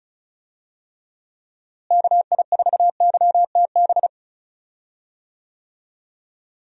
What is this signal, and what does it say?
Morse code, heard as a keyed tone, about 36 wpm, 700 Hz. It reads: KI4YTB
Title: USA Call Signs - 6 Characters - Repeated 2x - 35wpm